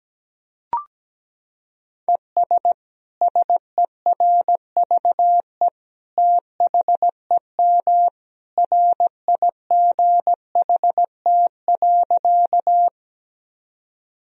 Morse code, heard as a keyed tone, about 17 wpm, 700 Hz. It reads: ES SERVE THEM RIGHT.
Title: Sentences from Top 400 Words and Abbreviations 17wpm